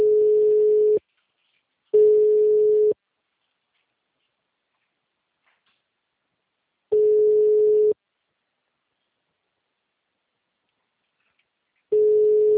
Ringback tone (Europe, 425 Hz).